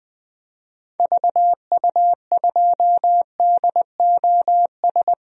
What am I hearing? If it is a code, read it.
Morse code, heard as a keyed tone, about 20 wpm, 700 Hz. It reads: VU2DOS
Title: DXCC-100 Call Signs - 6 Characters - Repeated 2x - 20wpm